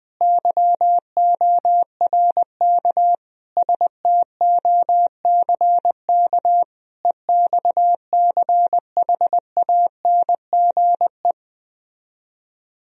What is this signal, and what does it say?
Morse code, heard as a keyed tone, about 20 wpm, 700 Hz. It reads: YORK STOCK EXCHANGE